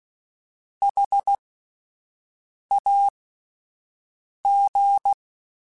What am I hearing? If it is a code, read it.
Morse code, heard as a keyed tone, about 16 wpm, spaced as for 5 wpm, 800 Hz. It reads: HAG
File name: Alphabetical / Alphabetical mp3s / Hs @5